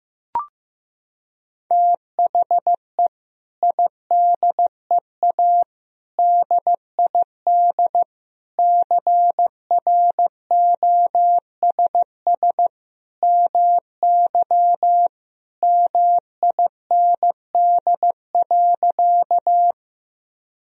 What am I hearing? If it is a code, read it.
Morse code, heard as a keyed tone, about 15 wpm, 700 Hz. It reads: THE IDEA DID CROSS MY MIND.
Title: Sentences from Top 1000 Words - Review Part 1 - 15wpm